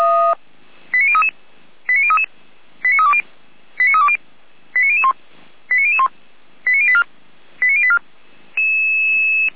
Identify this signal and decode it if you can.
Five-tone selective call (ZVEI), heard as quick runs of five tones, tones of about 70 ms. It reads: 89033 89033 89320 89320 89001 89001 89084 89084